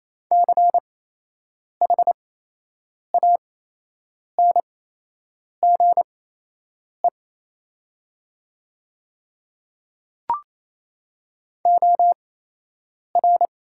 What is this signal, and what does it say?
Morse code, heard as a keyed tone, about 27 wpm, spaced as for 8 wpm, 700 Hz. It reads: CHANGE OR
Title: 8x Character Spacing - Top 300 Words - 28wpm